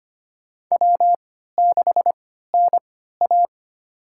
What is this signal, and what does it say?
Morse code, heard as a keyed tone, about 25 wpm, spaced as for 14 wpm, 700 Hz. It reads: W6NA
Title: USA Call Signs - 4 Characters - 3x Character Spacing - 25wpm